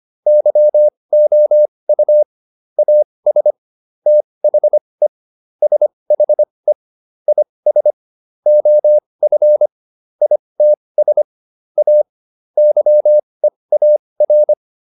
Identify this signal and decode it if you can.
Morse code, heard as a keyed tone, about 25 wpm, spaced as for 20 wpm, 600 Hz. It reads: YOU AS THE SHE IS OF ITS A YEAR